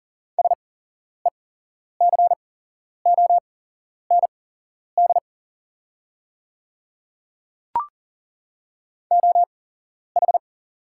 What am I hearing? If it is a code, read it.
Morse code, heard as a keyed tone, about 40 wpm, spaced as for 11 wpm, 700 Hz. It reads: SECOND OH
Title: 8x Character Spacing - Top 500 Words - 40wpm